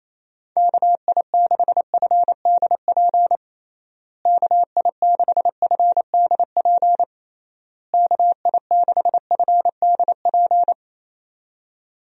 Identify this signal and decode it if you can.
Morse code, heard as a keyed tone, about 28 wpm, 700 Hz. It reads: KI6FDP KI6FDP KI6FDP
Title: USA Call Signs - 6 Characters - Repeated 3x - 28wpm